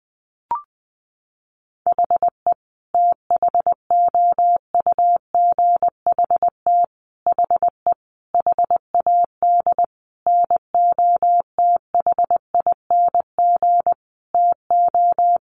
Morse code, heard as a keyed tone, about 20 wpm, 700 Hz. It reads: HE THOUGHT HE HAD NOTHING TO
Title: Sentences from Top 600 Words 20wpm